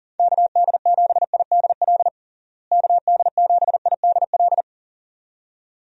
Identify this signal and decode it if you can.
Morse code, heard as a keyed tone, about 40 wpm, 700 Hz. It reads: KD7IDL KD7IDL
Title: USA Call Signs - 6 Characters - Repeated 2x - 40wpm